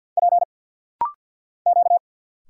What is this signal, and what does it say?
Morse code, heard as a keyed tone, about 52 wpm, 700 Hz. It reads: P Q
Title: Single Letters - Rapid-Fire 50wpm